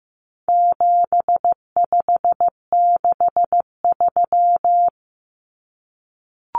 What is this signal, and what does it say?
Morse code, heard as a keyed tone, about 15 wpm, 700 Hz. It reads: 7563
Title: Numbers - 4-Digit 15wpm